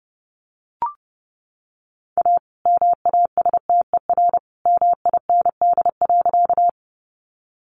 Morse code, heard as a keyed tone, about 30 wpm, 700 Hz. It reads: A MASTER MIND.
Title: Sentences from Top 1300 Words 30wpm